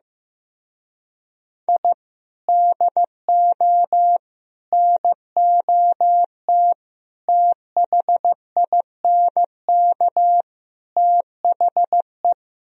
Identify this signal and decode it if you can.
Morse code, heard as a keyed tone, about 15 wpm, 700 Hz. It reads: I DO NOT THINK THE
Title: Speed-Racing - Sets of 5 Words 15wpm